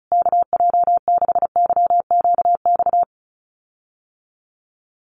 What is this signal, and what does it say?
Morse code, heard as a keyed tone, about 34 wpm, 700 Hz. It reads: KJ6YQX
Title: USA Call Signs Repeated 3x 35wpm